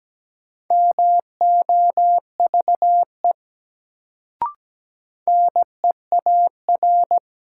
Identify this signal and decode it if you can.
Morse code, heard as a keyed tone, about 17 wpm, 700 Hz. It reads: MOVE NEAR
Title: Sentences from Top 200 Words - New Word Drill - 17wpm